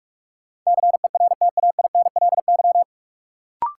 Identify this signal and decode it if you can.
Morse code, heard as a keyed tone, about 46 wpm, 700 Hz. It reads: CERTAINLY
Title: Sentences from Top 900 Words - New Word Drill - 45wpm